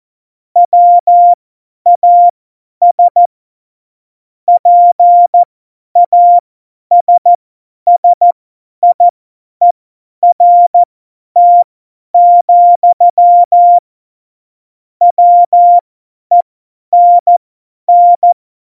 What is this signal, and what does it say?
Morse code, heard as a keyed tone, about 14 wpm, spaced as for 10 wpm, 700 Hz. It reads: WAS PASSIERT, WENN
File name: Morsekurs / Witze / Witz 04 Juli 17, Tempo50 F70